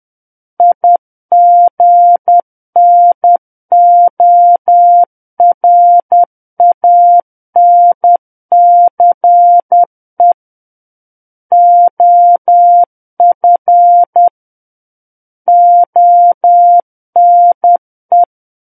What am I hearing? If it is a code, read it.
Morse code, heard as a keyed tone, about 10 wpm, 700 Hz. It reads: IGNORANCE OF ONE
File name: Book One / 01-Chapter 1 - 10 WPM